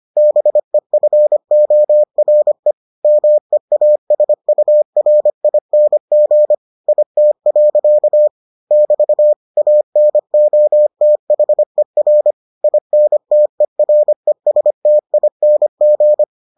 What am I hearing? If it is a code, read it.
Morse code, heard as a keyed tone, about 25 wpm, 600 Hz. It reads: BEFORE MEASURING IT. <BT> ANOTHER INTERESTING